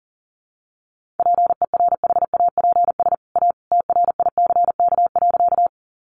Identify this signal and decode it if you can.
Morse code, heard as a keyed tone, about 40 wpm, 700 Hz. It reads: PERHAPS A TRICK.
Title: Sentences from Top 1500 Words 40wpm